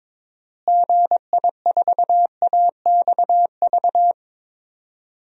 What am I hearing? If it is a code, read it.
Morse code, heard as a keyed tone, about 22 wpm, 700 Hz. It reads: GI4AXV
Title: DXCC-100 Call Signs - 6 Characters - Repeated 3x - 22wpm